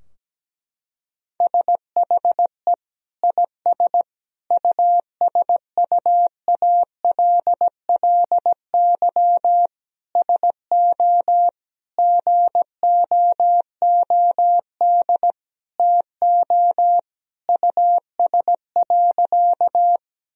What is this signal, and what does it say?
Morse code, heard as a keyed tone, about 17 wpm, 700 Hz. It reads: SHE IS USUALLY SO GOOD TO US.
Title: Sentences from Top 1000 Words - Review Part 3 - 17wpm